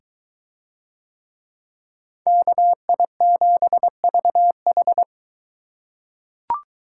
Morse code, heard as a keyed tone, about 22 wpm, 700 Hz. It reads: KI7VH